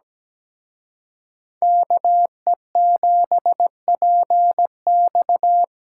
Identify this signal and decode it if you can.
Morse code, heard as a keyed tone, about 20 wpm, 700 Hz. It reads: KE7PX
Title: Speed Racing - USA Call Signs 17wpm